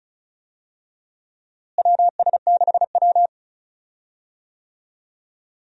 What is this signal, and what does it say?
Morse code, heard as a keyed tone, about 34 wpm, 700 Hz. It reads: WS6W
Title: USA Call Signs - 4 Characters - Repeated 3x - 35wpm